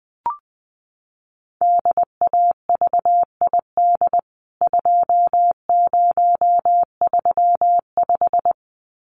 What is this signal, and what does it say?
Morse code, heard as a keyed tone, about 20 wpm, 700 Hz. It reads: DAVID 2035
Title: CWT Exchange 20wpm